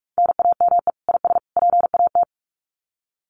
Morse code, heard as a keyed tone, about 44 wpm, 700 Hz. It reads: NAME IS PAT
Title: Common QSO Phrases 45wpm